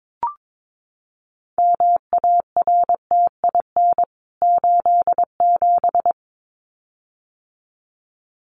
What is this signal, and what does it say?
Morse code, heard as a keyed tone, about 22 wpm, 700 Hz. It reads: MARTIN 87
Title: CWT Exchange 22wpm